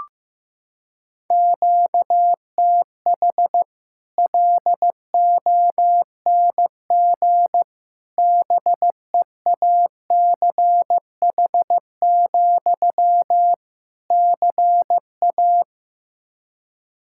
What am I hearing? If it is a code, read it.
Morse code, heard as a keyed tone, about 16 wpm, 700 Hz. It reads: QTH LONG BEACH, CA
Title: Practice QSO - Exchange 1 - Beginning - 15wpm